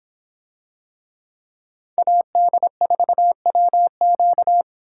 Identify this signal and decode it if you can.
Morse code, heard as a keyed tone, about 25 wpm, 700 Hz. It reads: AD4WQ